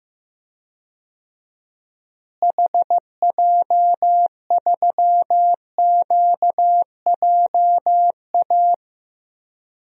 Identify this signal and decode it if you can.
Morse code, heard as a keyed tone, about 15 wpm, 700 Hz. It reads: HJ3QJA
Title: DXCC-100 Call Signs - 6 Characters - Repeated 2x - 15wpm